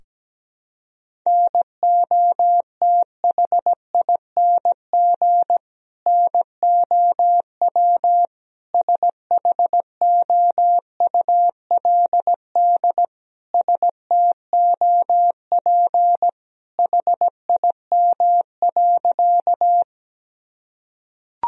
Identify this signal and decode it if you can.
Morse code, heard as a keyed tone, about 17 wpm, 700 Hz. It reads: NOTHING NOW SHOULD STOP HIM.
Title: Sentences from Top 1000 Words - Review Part 1 - 17wpm